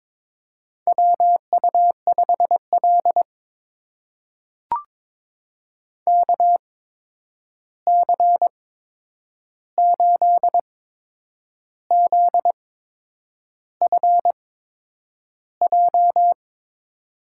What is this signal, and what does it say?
Morse code, heard as a keyed tone, about 21 wpm, spaced as for 8 wpm, 700 Hz. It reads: WU5L KC8ZFJ
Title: USA Call Signs - 8x Character Spacing - 22wpm